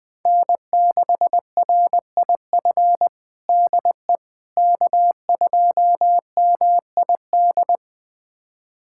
Morse code, heard as a keyed tone, about 20 wpm, 700 Hz. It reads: N6RIF DE K2MID
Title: USA Callsign DE Callsign - 5 Characters - 20wpm